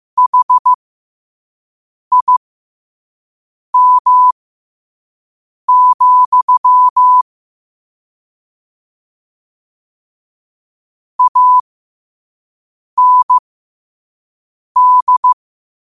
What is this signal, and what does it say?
Morse code, heard as a keyed tone, about 15 wpm, spaced as for 5 wpm, 1000 Hz. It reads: HIM, AND
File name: 2026-02-15-5wpm